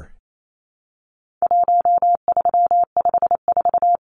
Morse code, heard as a keyed tone, about 28 wpm, 700 Hz. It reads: 1354
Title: Numbers - 4-Digit 28wpm